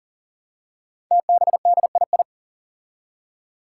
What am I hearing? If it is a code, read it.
Morse code, heard as a keyed tone, about 40 wpm, 700 Hz. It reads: TBDII
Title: Five Challenging Letter-Number Combinations 40wpm